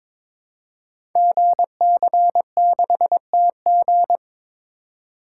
Morse code, heard as a keyed tone, about 22 wpm, 700 Hz. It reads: GC6TG
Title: Code Groups - 5 Letter:Numbers - 22wpm